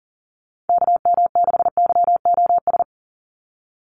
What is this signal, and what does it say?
Morse code, heard as a keyed tone, about 40 wpm, 700 Hz. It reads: KM6YOS
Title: USA Call Signs 40wpm